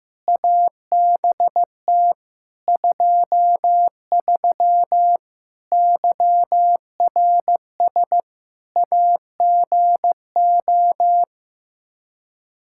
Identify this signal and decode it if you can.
Morse code, heard as a keyed tone, about 16 wpm, 700 Hz. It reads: ABT 23 YRS AGO
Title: K1USN SST QSO Practice 15wpm